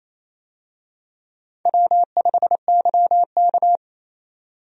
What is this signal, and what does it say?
Morse code, heard as a keyed tone, about 28 wpm, 700 Hz. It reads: W5YK
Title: USA Call Signs - 4 Characters - 28wpm